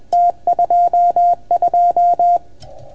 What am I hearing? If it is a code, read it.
Morse code, heard as a keyed tone, about 28 wpm, 690 Hz. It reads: T22
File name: Archiv / CUC310 / OK1FGD CUC 310 pileup